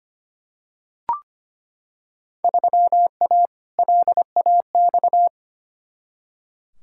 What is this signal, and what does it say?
Morse code, heard as a keyed tone, about 25 wpm, 700 Hz. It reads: 3A LAX
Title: Field Day Exchanges - 25wpm